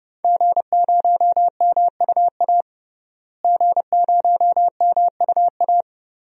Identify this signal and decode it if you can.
Morse code, heard as a keyed tone, about 32 wpm, 700 Hz. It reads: G0MUA G0MUA
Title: DXCC-100 Call Signs - 5 Characters - Repeated 3x - 30wpm